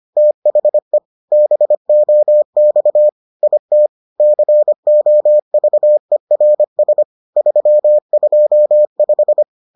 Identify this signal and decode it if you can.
Morse code, heard as a keyed tone, about 25 wpm, 600 Hz. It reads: THE BOX IT COVERS 325